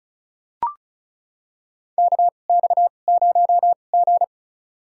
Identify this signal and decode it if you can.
Morse code, heard as a keyed tone, about 36 wpm, spaced as for 25 wpm, 700 Hz. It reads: KX0G
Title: USA Call Signs - 2x Character Spacing - 35wpm